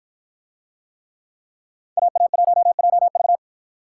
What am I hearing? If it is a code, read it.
Morse code, heard as a keyed tone, about 52 wpm, 700 Hz. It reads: AA1JV